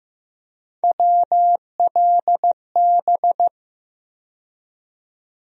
Morse code, heard as a keyed tone, about 15 wpm, 700 Hz. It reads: WLB
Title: USA Call Sign Postfixes - 3 Letters 15wpm